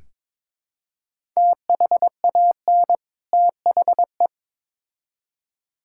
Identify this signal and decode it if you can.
Morse code, heard as a keyed tone, about 23 wpm, 700 Hz. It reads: THAN THE
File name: Build A Phrase - 2-Word Phrases - 22wpm